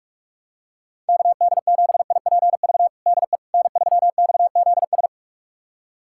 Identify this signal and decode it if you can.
Morse code, heard as a keyed tone, about 46 wpm, 700 Hz. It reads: KD7IPV DE N3XZS
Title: USA Callsign DE Callsign 45wpm